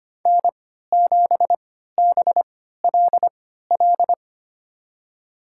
Morse code, heard as a keyed tone, about 25 wpm, spaced as for 14 wpm, 700 Hz. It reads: N7BLL
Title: USA Call Signs - 3x Character Spacing - 25wpm